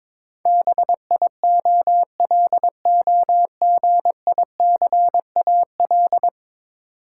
Morse code, heard as a keyed tone, about 22 wpm, 700 Hz. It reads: BIOLOGICAL
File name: practice / 22wpm / 10 Letter Words - First 3 Letters Given - 22wpm